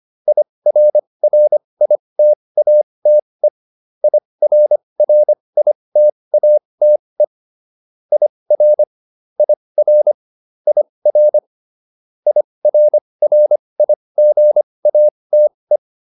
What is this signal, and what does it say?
Morse code, heard as a keyed tone, about 25 wpm, spaced as for 20 wpm, 600 Hz. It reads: IRRITATE IRRITATE IR IR IR IRRIGATE